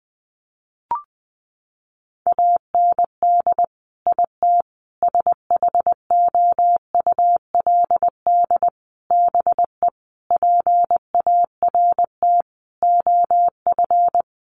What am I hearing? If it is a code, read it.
Morse code, heard as a keyed tone, about 20 wpm, 700 Hz. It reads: AND IT SHOULD BE PART OF
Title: Sentences from Top 1500 Words 20wpm